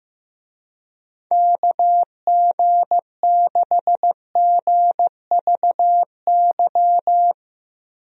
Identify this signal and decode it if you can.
Morse code, heard as a keyed tone, about 16 wpm, 700 Hz. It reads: KG6GVY
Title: USA Call Signs - 6 Characters - Repeated 3x - 15wpm